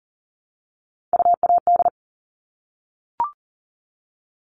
Morse code, heard as a keyed tone, about 40 wpm, 700 Hz. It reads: UAD
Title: All Cut Numbers - 3-Digit 40wpm